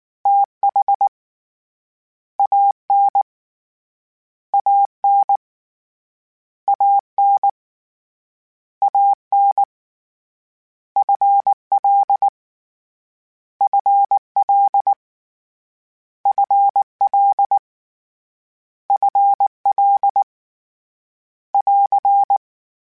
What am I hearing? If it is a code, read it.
Morse code, heard as a keyed tone, about 19 wpm, 800 Hz. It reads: T H A N A N A N A N F L F L F L F L <AR>
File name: Contrasts / Contrasts-mp3s / 2LtrContrasts @18